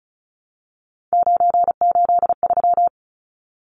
Morse code, heard as a keyed tone, about 34 wpm, 700 Hz. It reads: 983
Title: Numbers - 3-Digit 35wpm